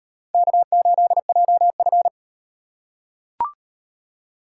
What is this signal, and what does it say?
Morse code, heard as a keyed tone, about 38 wpm, 700 Hz. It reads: K8JF